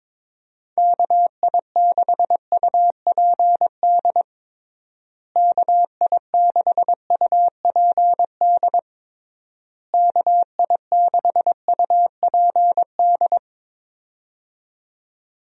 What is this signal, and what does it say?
Morse code, heard as a keyed tone, about 22 wpm, 700 Hz. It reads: KI6UPD KI6UPD KI6UPD
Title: USA Call Signs - 6 Characters - Repeated 3x - 22wpm